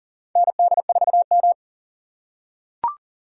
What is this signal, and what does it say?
Morse code, heard as a keyed tone, about 39 wpm, 690 Hz. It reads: ND4M
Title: USA Call Signs - 4 Characters - 2x Character Spacing - 40wpm